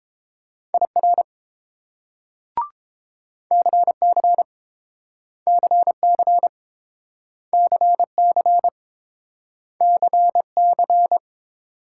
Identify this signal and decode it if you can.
Morse code, heard as a keyed tone, about 29 wpm, 700 Hz. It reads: IR CC CC CC CC